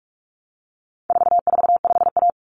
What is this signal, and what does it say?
Morse code, heard as a keyed tone, about 44 wpm, 700 Hz. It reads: 445A
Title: Common Cut Numbers - 4-Digit 45wpm